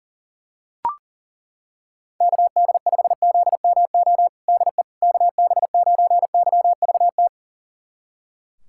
Morse code, heard as a keyed tone, about 40 wpm, 700 Hz. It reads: KD5ZMO DE KB9YVT
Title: USA Callsign DE Callsign - 6 Characters - 40wpm